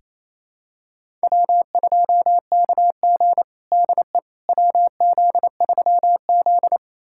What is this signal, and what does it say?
Morse code, heard as a keyed tone, about 28 wpm, 700 Hz. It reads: W2KG DE WZ3Z
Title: USA Callsign DE Callsign - 4 Characters - 28wpm